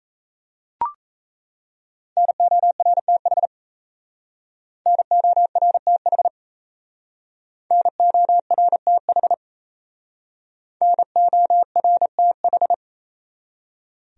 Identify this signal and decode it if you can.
Morse code, heard as a keyed tone, about 34 wpm, 700 Hz. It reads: NORTH NORTH NORTH NORTH
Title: Speed-Racing Top 300 Words 28wpm